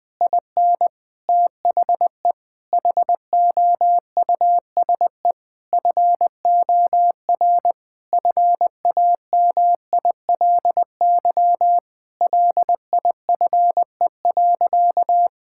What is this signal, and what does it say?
Morse code, heard as a keyed tone, about 20 wpm, 700 Hz. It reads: IN THE HOUSE FOR FAMILY LIFE.
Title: Sentences from Top 1000 Words - Review Part 5 - 20wpm